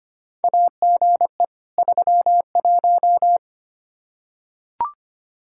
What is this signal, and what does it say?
Morse code, heard as a keyed tone, about 25 wpm, 700 Hz. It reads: AGE 31
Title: K1USN SST QSO Practice 25wpm